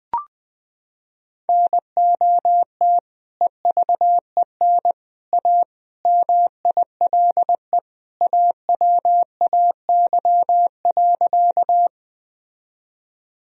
Morse code, heard as a keyed tone, about 20 wpm, 700 Hz. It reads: NOT EVEN A MILE AWAY.
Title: Sentences from Top 1000 Words - Review Part 1 - 20wpm